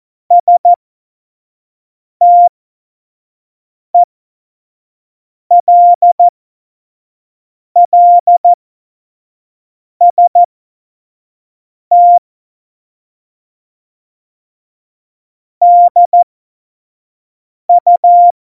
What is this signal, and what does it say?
Morse code, heard as a keyed tone, about 14 wpm, spaced as for 5 wpm, 700 Hz. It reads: STELLST DU
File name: Morsekurs / Witze / Witz 21. Mai 2019, 25 F70